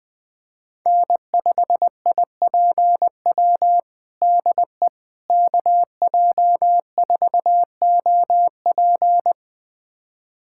Morse code, heard as a keyed tone, about 21 wpm, 700 Hz. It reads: N5IPW DE KJ4OP
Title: USA Callsign DE Callsign - 5 Characters - 20wpm